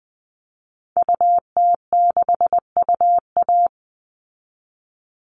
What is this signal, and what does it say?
Morse code, heard as a keyed tone, about 20 wpm, 700 Hz. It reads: UT6UA
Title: Call Signs Repeated 2x - 20wpm